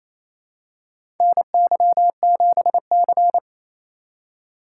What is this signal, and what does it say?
Morse code, heard as a keyed tone, about 28 wpm, 700 Hz. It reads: NY7C
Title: USA Call Signs - 4 Characters - 4x Character Spacing - 28wpm